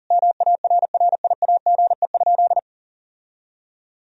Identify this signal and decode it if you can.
Morse code, heard as a keyed tone, about 40 wpm, 700 Hz. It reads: MARRIAGE?
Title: Sentences from Top 1000 Words - Review Part 4 - 40wpm